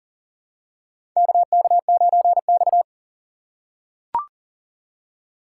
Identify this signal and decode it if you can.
Morse code, heard as a keyed tone, about 40 wpm, 700 Hz. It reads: KK9X
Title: USA Call Signs - 4 Characters - 3x Character Spacing - 40wpm